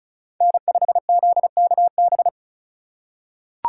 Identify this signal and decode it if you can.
Morse code, heard as a keyed tone, about 34 wpm, 700 Hz. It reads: N5ZKB
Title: USA Call Signs - 5 Characters - 4x Character Spacing - 35wpm